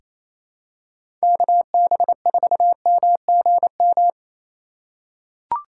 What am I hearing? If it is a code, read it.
Morse code, heard as a keyed tone, about 28 wpm, 700 Hz. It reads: KB4MGM
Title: USA Call Signs - 6 Characters - 3x Character Spacing - 28wpm